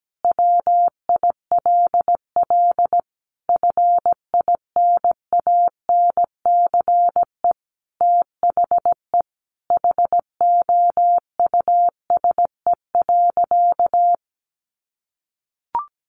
Morse code, heard as a keyed tone, about 17 wpm, 700 Hz. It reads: WILL FINANCE THE HOUSE.
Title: Sentences from Top 1500 Words 17wpm